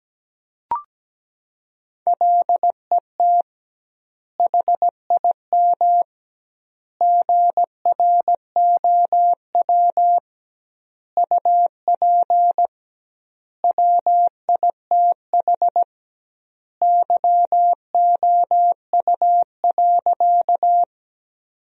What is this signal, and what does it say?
Morse code, heard as a keyed tone, about 17 wpm, 700 Hz. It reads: LET HIM GROW UP WITH YOU.
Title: 2x Word-Spacing - Sentences from Top 300 Words - 17wpm